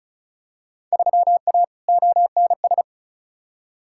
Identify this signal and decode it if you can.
Morse code, heard as a keyed tone, about 34 wpm, 700 Hz. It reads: 3A ONS
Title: Field Day Exchanges - 35wpm